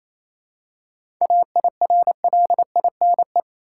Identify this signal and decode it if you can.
Morse code, heard as a keyed tone, about 28 wpm, 700 Hz. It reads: AIRLINE
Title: Seven Letter Words - 28wpm